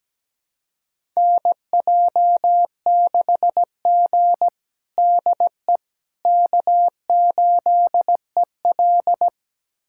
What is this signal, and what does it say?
Morse code, heard as a keyed tone, about 17 wpm, 700 Hz. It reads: NJ6G DE K8EL
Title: USA Callsign DE Callsign - 4 Characters - 17wpm